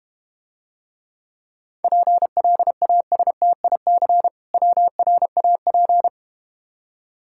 Morse code, heard as a keyed tone, about 31 wpm, 700 Hz. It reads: PLASTIC WRAP
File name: Easy Words - Speed-Racing with Hints - 25wpm